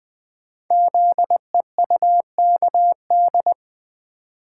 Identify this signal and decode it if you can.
Morse code, heard as a keyed tone, about 20 wpm, 700 Hz. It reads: ZEUKD
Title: Code Groups - 5 Letters - 20wpm